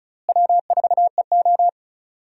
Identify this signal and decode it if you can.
Morse code, heard as a keyed tone, about 36 wpm, 700 Hz. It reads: W4EO
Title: USA Call Signs - 4 Characters - Repeated 3x - 35wpm